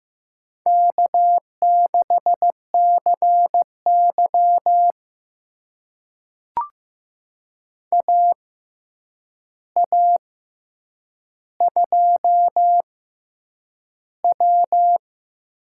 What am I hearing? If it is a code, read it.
Morse code, heard as a keyed tone, about 15 wpm, spaced as for 8 wpm, 700 Hz. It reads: K6CY AA2W
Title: USA Call Signs - 4 Characters - 6x Character Spacing - 15wpm